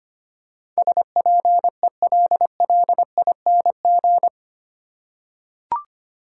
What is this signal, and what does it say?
Morse code, heard as a keyed tone, about 25 wpm, 690 Hz. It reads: SPELLING
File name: 8 Letter Words - First 3 Letters Given - 25wpm